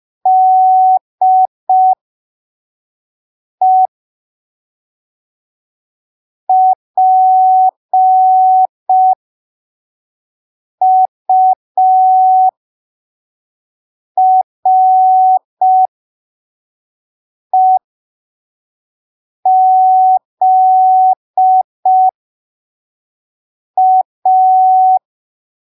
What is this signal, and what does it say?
Morse code, heard as a keyed tone, about 5 wpm, spaced as for 3 wpm, 750 Hz. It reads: DE PUREZA
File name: arquivos / texto / texto 11 05ppm